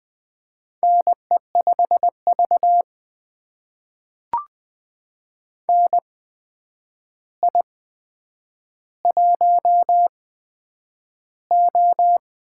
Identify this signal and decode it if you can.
Morse code, heard as a keyed tone, about 20 wpm, spaced as for 9 wpm, 700 Hz. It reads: NE5V NI1O
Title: USA Call Signs - 4 Characters - 8x Character Spacing - 20wpm